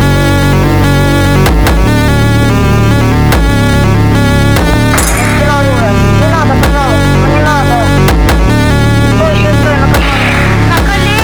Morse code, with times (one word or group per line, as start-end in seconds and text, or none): none